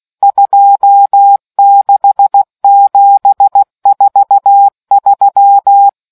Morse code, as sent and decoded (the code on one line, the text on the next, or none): ..--- -.... --... ....- ...--
26743